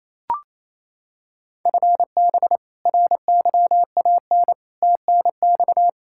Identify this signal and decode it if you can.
Morse code, heard as a keyed tone, about 28 wpm, 700 Hz. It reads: FB RYAN TNX
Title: Practice QSO - Exchange 3 - More Info - 28wpm